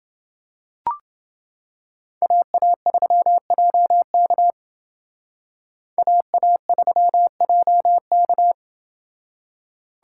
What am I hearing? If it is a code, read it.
Morse code, heard as a keyed tone, about 28 wpm, 700 Hz. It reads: AA3JK AA3JK